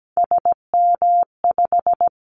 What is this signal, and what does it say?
Morse code, heard as a keyed tone, about 17 wpm, 700 Hz. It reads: SM5
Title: Callsign Prefixes for Top 25 DXCC - 17wpm